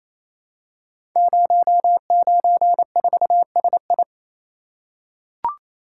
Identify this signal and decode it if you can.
Morse code, heard as a keyed tone, about 27 wpm, 700 Hz. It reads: 094SI
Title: Code Groups - 5 Letter:Numbers - 28wpm